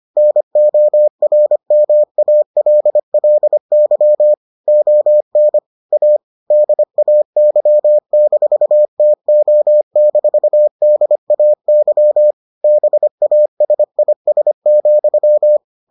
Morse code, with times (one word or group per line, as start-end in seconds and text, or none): 0.2-4.3 NORMALLY
4.7-5.6 ON
5.9-6.2 A
6.5-12.3 DAY-TO-DAY
12.6-15.6 BASIS,